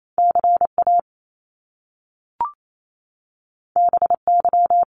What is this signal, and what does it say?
Morse code, heard as a keyed tone, about 28 wpm, 700 Hz. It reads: CA BY
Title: Top Two Letter Combinations in Words 28wpm